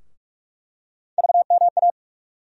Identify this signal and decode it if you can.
Morse code, heard as a keyed tone, about 46 wpm, 700 Hz. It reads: VMA
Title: USA Call Sign Postfixes - 3 Letters 45wpm